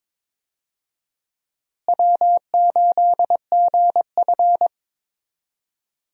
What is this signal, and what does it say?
Morse code, heard as a keyed tone, about 22 wpm, 700 Hz. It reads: W8GF
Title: USA Call Signs - 4 Characters - Repeated 2x - 22wpm